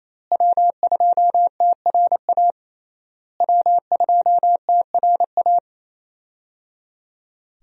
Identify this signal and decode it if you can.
Morse code, heard as a keyed tone, about 28 wpm, 700 Hz. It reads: W2TRA W2TRA
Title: USA Call Signs - 5 Characters - Repeated 3x - 28wpm